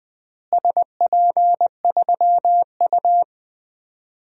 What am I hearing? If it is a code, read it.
Morse code, heard as a keyed tone, about 20 wpm, 700 Hz. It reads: SP3U
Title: DXCC-100 Call Signs - 4 Characters - Repeated 2x - 20wpm